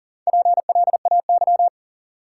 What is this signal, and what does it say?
Morse code, heard as a keyed tone, about 40 wpm, 700 Hz. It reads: PLAY
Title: Sentences from Top 200 Words - New Word Drill - 40wpm